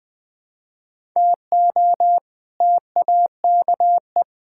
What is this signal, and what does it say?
Morse code, heard as a keyed tone, about 20 wpm, 700 Hz. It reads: TO TAKE
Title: Build A Phrase - 4-Word Phrases - 20wpm